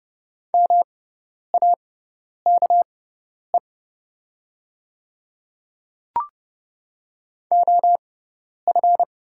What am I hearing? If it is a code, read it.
Morse code, heard as a keyed tone, about 30 wpm, spaced as for 10 wpm, 690 Hz. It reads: MAKE OF
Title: 6x Character Spacing - Top 100 Words - 30wpm